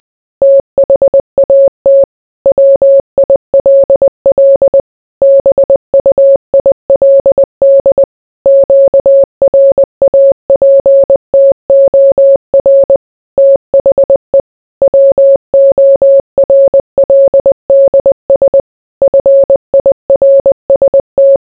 Morse code, heard as a keyed tone, about 20 wpm, 550 Hz. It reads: THAT WILL BUILD QRAPTOR THE WORLDS FIRST